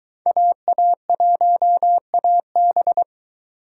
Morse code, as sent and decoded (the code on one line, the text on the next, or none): .- .- .---- .- -...
AA1AB